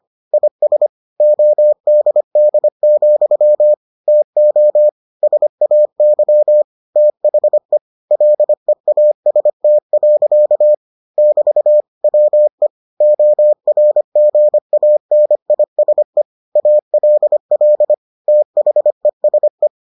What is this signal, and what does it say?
Morse code, heard as a keyed tone, about 25 wpm, 600 Hz. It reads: IS ODD, TO SAY THE LEAST. <BT> WE ORGANISE ALL THESE